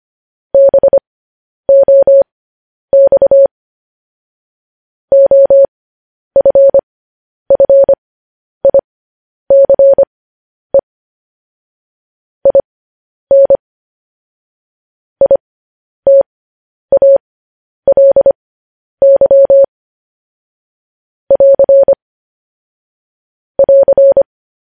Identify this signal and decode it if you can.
Morse code, heard as a keyed tone, about 25 wpm, spaced as for 10 wpm, 550 Hz. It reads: BOX OFFICE IN ITALY <AR> <AR>
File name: podcast / News headlines-1x-25x10